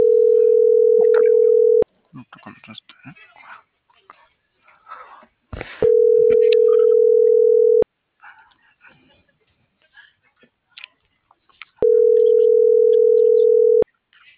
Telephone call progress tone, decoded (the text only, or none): ringback tone